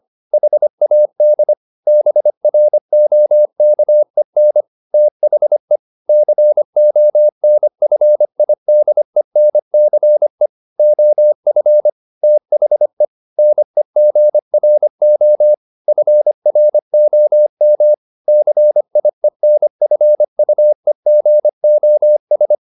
Morse code, heard as a keyed tone, about 25 wpm, 600 Hz. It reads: HAD BROKEN THE CONFIDENCE OF THE NEGRO FROM CIENFUEGOS